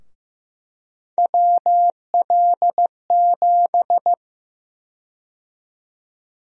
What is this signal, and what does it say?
Morse code, heard as a keyed tone, about 15 wpm, 700 Hz. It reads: WL7
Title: USA Call Sign Prefixes - 15wpm